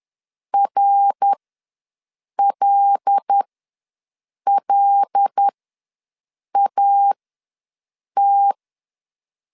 Morse code, heard as a keyed tone, about 10 wpm, spaced as for 6 wpm, 800 Hz. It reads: RLLAT
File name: LEZIONI / 5 LBVC / 5 L ws 10wpm 3636